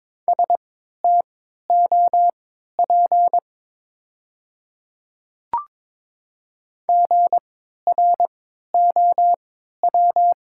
Morse code, heard as a keyed tone, about 22 wpm, spaced as for 12 wpm, 700 Hz. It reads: STOP GROW
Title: 3x Character Spacing - Top 300 Words - 22wpm